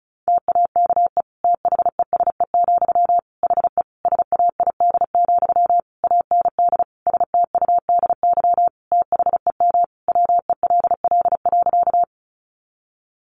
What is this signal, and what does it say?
Morse code, heard as a keyed tone, about 34 wpm, 700 Hz. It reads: TAKE THESE, HE SAID, AND STUDY THEM WELL.
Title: Sentences from Top 300 Words 35wpm